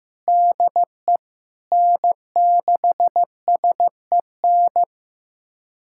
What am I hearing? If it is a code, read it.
Morse code, heard as a keyed tone, about 15 wpm, 700 Hz. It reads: DE N6SEN
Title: USA Callsign DE Callsign - 5 Characters - 15wpm